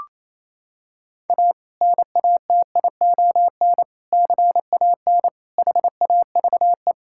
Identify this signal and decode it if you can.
Morse code, heard as a keyed tone, about 28 wpm, 700 Hz. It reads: A NATION CAN HAVE
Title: Sentences from Top 1000 Words - Review Part 6 - 28wpm